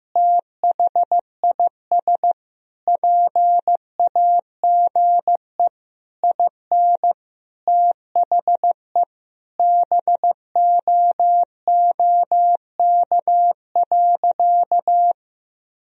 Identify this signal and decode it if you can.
Morse code, heard as a keyed tone, about 15 wpm, 700 Hz. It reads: THIS PAGE IN THE BOOK.
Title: Sentences from Top 1000 Words - Review Part 1 - 15wpm